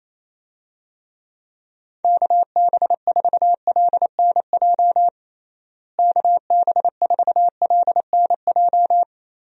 Morse code, heard as a keyed tone, about 27 wpm, 700 Hz. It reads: KB4LNJ KB4LNJ
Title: USA Call Signs - 6 Characters - Repeated 3x - 28wpm